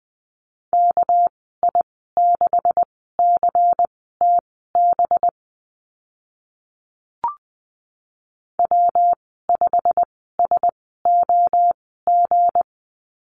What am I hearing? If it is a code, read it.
Morse code, heard as a keyed tone, about 20 wpm, spaced as for 14 wpm, 700 Hz. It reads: KI6CTB W5SOG
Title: 213 Lesson 34g - Number 8 - Call Signs 20wpm